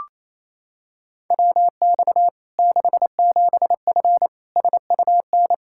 Morse code, heard as a keyed tone, about 27 wpm, 700 Hz. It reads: WX 67F SUN
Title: K1USN SST QSO Practice 28wpm